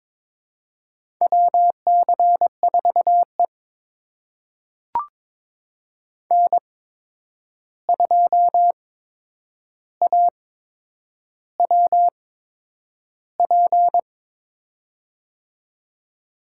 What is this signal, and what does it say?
Morse code, heard as a keyed tone, about 22 wpm, spaced as for 9 wpm, 700 Hz. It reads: WC4E N2AWP
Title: USA Call Signs - 8x Character Spacing - 22wpm